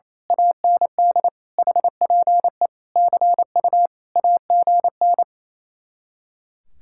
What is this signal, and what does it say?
Morse code, heard as a keyed tone, about 26 wpm, 700 Hz. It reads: AND HPE CU AGN
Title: Speed-Racing Common QSO Phrases - 28wpm